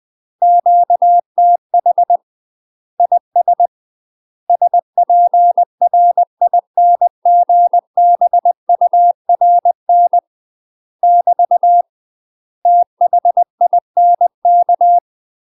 Morse code, heard as a keyed tone, about 20 wpm, 700 Hz. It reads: QTH IS SPRINGBURN <BT> THINK
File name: UK MASTER QSOs TRAINING / 20WPM / UK MASTER QSOs 0064